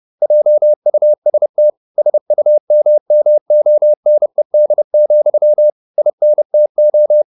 Morse code, heard as a keyed tone, about 30 wpm, 600 Hz. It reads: JUST SUMMONED, INTO